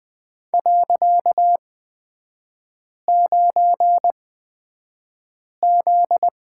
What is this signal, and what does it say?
Morse code, heard as a keyed tone, about 20 wpm, spaced as for 5 wpm, 700 Hz. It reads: .9Z
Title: 255 Lesson 41e - Period - Three Characters 20wpm